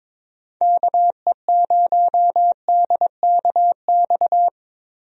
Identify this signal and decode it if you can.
Morse code, heard as a keyed tone, about 22 wpm, 700 Hz. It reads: KE0DKX
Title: USA Call Signs - 6 Characters - 22wpm